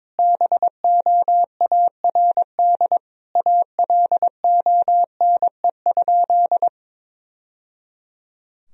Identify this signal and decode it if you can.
Morse code, heard as a keyed tone, about 21 wpm, 700 Hz. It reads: BOARD ALONE?
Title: Sentences from Top 1000 Words - Review Part 4 - 22wpm